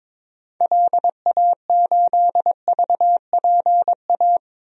Morse code, heard as a keyed tone, about 22 wpm, 700 Hz. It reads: LA8VPA
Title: DXCC-100 Call Signs - 6 Characters - 22wpm